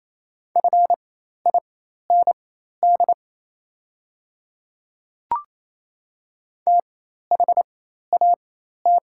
Morse code, heard as a keyed tone, about 28 wpm, spaced as for 13 wpm, 700 Hz. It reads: FIND THAT
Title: 4x Character Spacing - Top 100 Words - 28wpm